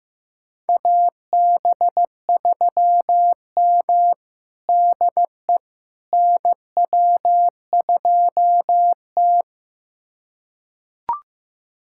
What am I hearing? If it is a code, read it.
Morse code, heard as a keyed tone, about 15 wpm, 700 Hz. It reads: AB3M DE NW2T
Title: USA Callsign DE Callsign - 4 Characters - 15wpm